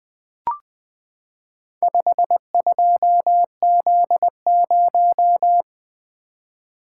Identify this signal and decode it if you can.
Morse code, heard as a keyed tone, about 20 wpm, 700 Hz. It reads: 52Z0
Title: Four Letter-Number Combinations - 20wpm